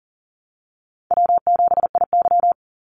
Morse code, heard as a keyed tone, about 40 wpm, 700 Hz. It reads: W7IY
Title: Easy Call Signs Repeated 2x - 40wpm